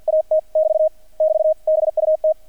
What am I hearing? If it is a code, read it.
Morse code, heard as a keyed tone, about 50 wpm, 630 Hz. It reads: AT <BT> <BT> BUT
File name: archived / CRSnet